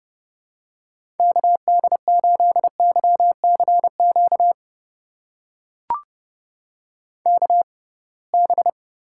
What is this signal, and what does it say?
Morse code, heard as a keyed tone, about 31 wpm, spaced as for 23 wpm, 700 Hz. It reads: KD8YCQ KB
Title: USA Call Signs - 6 Characters - 6x Character Spacing - 30wpm